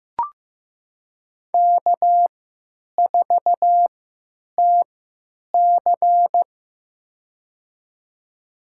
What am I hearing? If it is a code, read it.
Morse code, heard as a keyed tone, about 15 wpm, spaced as for 9 wpm, 700 Hz. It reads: K4TC
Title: USA Call Signs - 4 Characters - 3x Character Spacing - 15wpm